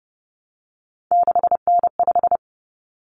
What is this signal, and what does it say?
Morse code, heard as a keyed tone, about 30 wpm, 700 Hz. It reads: 6N5
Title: Common Cut Numbers - 3-Digit 30wpm